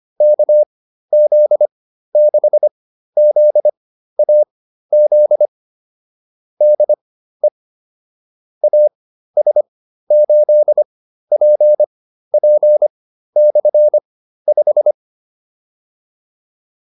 Morse code, heard as a keyed tone, about 25 wpm, spaced as for 13 wpm, 600 Hz. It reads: KZ6ZAZ DE AS8PP/5